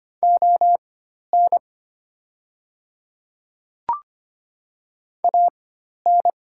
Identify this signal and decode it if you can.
Morse code, heard as a keyed tone, about 25 wpm, spaced as for 12 wpm, 700 Hz. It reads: ON AN
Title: 4x Character Spacing - Top 100 Words - 25wpm